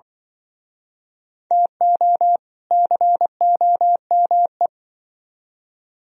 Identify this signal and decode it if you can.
Morse code, heard as a keyed tone, about 24 wpm, 700 Hz. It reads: TO COME
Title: Speed-Racing - Sets of 2 Words 20wpm